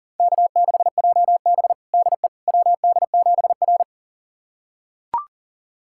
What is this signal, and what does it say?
Morse code, heard as a keyed tone, about 40 wpm, 690 Hz. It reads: K6JB DE WD7R
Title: USA Callsign DE Callsign - 4 Characters - 40wpm